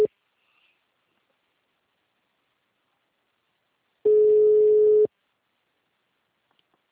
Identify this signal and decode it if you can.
Ringback tone (Europe, 425 Hz).